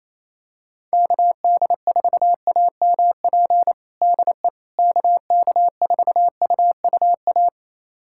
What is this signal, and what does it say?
Morse code, heard as a keyed tone, about 28 wpm, 700 Hz. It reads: KD4AMP DE KK4UUA
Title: USA Callsign DE Callsign 28wpm